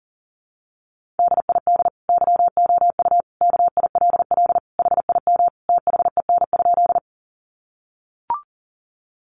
Morse code, heard as a keyed tone, about 40 wpm, 700 Hz. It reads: DID YOU KILL HIM THEN?
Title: Sentences from Top 1400 Words 40wpm